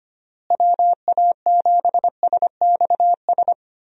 Morse code, heard as a keyed tone, about 25 wpm, 700 Hz. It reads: WA7SXS
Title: USA Call Signs - 6 Characters - Repeated 3x - 25wpm